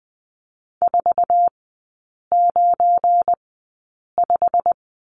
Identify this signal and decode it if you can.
Morse code, heard as a keyed tone, about 20 wpm, spaced as for 8 wpm, 700 Hz. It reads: 495
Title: 178 Lesson 29e - Slash - Three Characters 20wpm